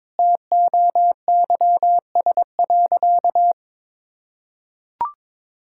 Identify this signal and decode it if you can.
Morse code, heard as a keyed tone, about 22 wpm, 700 Hz. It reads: TOYS.
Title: Pangram Challenge v2 - 22wpm